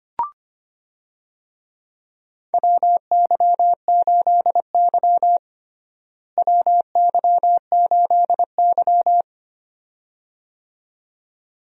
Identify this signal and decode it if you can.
Morse code, heard as a keyed tone, about 25 wpm, 700 Hz. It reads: WY8Y WY8Y
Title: USA Call Signs - 4 Characters - Repeated 2x - 25wpm